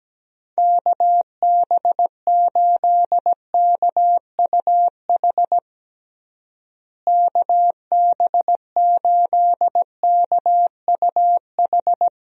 Morse code, heard as a keyed tone, about 17 wpm, 700 Hz. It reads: KB8KUH KB8KUH
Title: USA Call Signs - 6 Characters - Repeated 2x - 17wpm